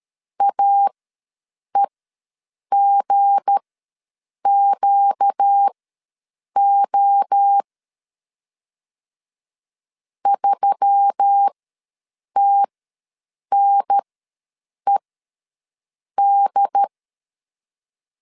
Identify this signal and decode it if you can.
Morse code, heard as a keyed tone, about 13 wpm, spaced as for 7 wpm, 800 Hz. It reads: AEGQO 3TNED
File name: LEZIONI / 7 GF9Z / 7 GF9 ws 12wpm 3344